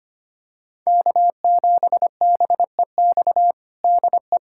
Morse code, heard as a keyed tone, about 25 wpm, 700 Hz. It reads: K7BEX DE
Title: USA Callsign DE Callsign - 5 Characters - 25wpm